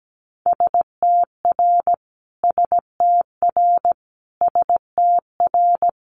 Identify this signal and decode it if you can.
Morse code, heard as a keyed tone, about 17 wpm, 700 Hz. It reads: STR STR STR
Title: Multi-letter Phonemes Repeated 3x 17wpm